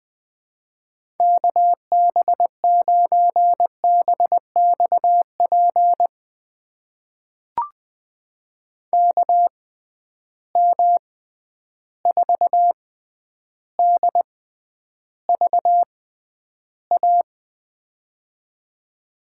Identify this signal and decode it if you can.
Morse code, heard as a keyed tone, about 20 wpm, spaced as for 10 wpm, 700 Hz. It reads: KB9BXP KM4DVA
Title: USA Call Signs - 6 Characters - 6x Character Spacing - 20wpm